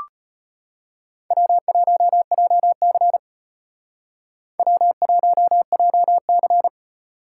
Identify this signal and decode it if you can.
Morse code, heard as a keyed tone, about 32 wpm, 700 Hz. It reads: W1JC W1JC